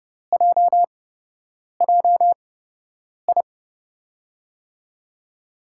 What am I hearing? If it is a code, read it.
Morse code, heard as a keyed tone, about 30 wpm, spaced as for 8 wpm, 700 Hz. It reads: JJI
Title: USA Call Signs - 6 Characters - 8x Character Spacing - 30wpm